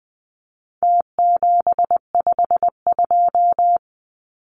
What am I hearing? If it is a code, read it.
Morse code, heard as a keyed tone, about 20 wpm, 700 Hz. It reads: T752
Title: Numbers - Contest Style - 4-Digit 20wpm